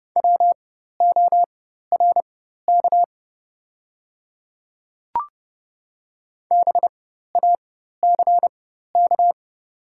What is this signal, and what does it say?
Morse code, heard as a keyed tone, about 30 wpm, spaced as for 14 wpm, 700 Hz. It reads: WORK BACK
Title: 4x Character Spacing - Top 200 Words - 30wpm